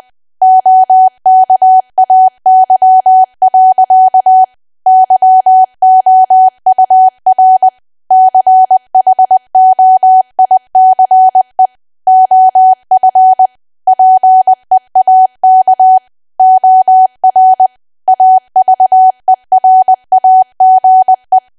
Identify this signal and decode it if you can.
Morse code, heard as a keyed tone, about 20 wpm, 750 Hz. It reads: OKAY. YOUR CHOICE OF PEAK OR AVERAGE
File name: cwa / W1AW-Level3 / 20WPM